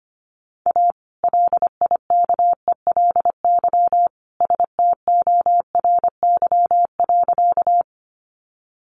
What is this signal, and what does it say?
Morse code, heard as a keyed tone, about 25 wpm, 700 Hz. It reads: A LIKELY STORY.
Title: Sentences from Top 600 Words 25wpm